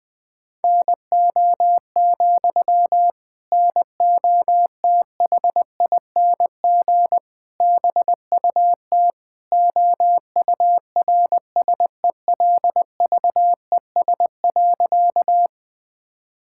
Morse code, heard as a keyed tone, about 21 wpm, 700 Hz. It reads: NO, NOTHING BUT OURSELVES.
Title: Sentences from Top 1600 Words 20wpm